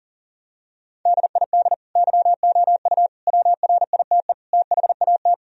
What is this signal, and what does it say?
Morse code, heard as a keyed tone, about 40 wpm, 700 Hz. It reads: DID YOU WRITE THAT
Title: Sentences from Top 1000 Words - Review Part 3 - 40wpm